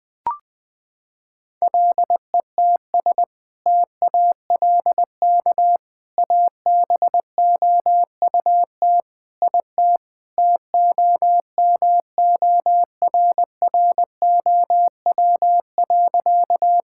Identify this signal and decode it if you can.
Morse code, heard as a keyed tone, about 20 wpm, 700 Hz. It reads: LETS TALK ABOUT IT TOMORROW.
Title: Sentences with 2-Word Phrases - 20wpm